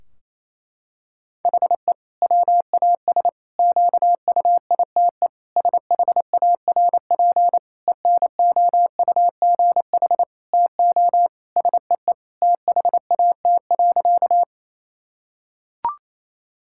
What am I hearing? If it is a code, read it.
Morse code, heard as a keyed tone, about 27 wpm, 700 Hz. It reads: HE WAS QUITE SHARP ENOUGH TO SEE THAT.
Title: Sentences from Top 1600 Words 28wpm